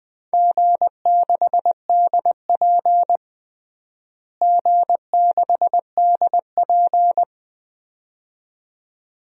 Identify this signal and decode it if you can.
Morse code, heard as a keyed tone, about 20 wpm, 700 Hz. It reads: G6DP G6DP
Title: DXCC-100 Call Signs - 4 Characters - Repeated 2x - 20wpm